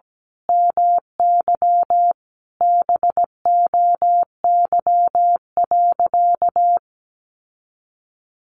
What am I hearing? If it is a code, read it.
Morse code, heard as a keyed tone, about 17 wpm, 700 Hz. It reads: MY BOY.
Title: Sentences from Top 800 Words 17wpm